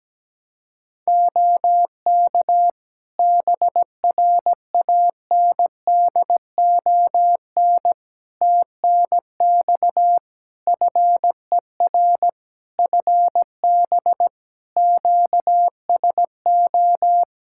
Morse code, heard as a keyed tone, about 17 wpm, 700 Hz. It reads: OK BRANDON TNX FER FB QSO
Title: K1USN SST QSO Practice 17wpm